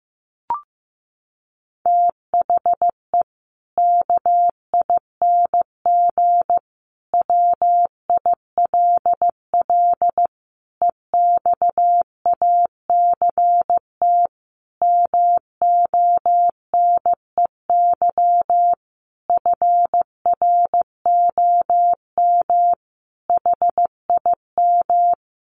Morse code, heard as a keyed tone, about 15 wpm, 700 Hz. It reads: THE KING WILL EXACT MONEY FROM HIM
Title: Sentences from Top 1300 Words 15wpm